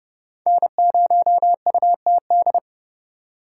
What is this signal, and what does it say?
Morse code, heard as a keyed tone, about 37 wpm, 700 Hz. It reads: N0UTD
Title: Speed Racing - USA 1x3 Call Signs 30wpm